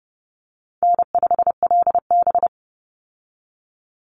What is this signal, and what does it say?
Morse code, heard as a keyed tone, about 30 wpm, 700 Hz. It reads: N5LB
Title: Easy Call Signs Repeated 3x - 30wpm